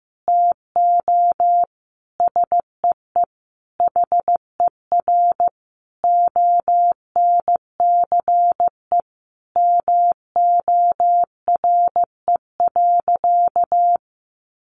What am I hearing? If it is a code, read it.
Morse code, heard as a keyed tone, about 15 wpm, 700 Hz. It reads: TO SEE HER ONCE MORE.
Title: Sentences from Top 500 Words 15wpm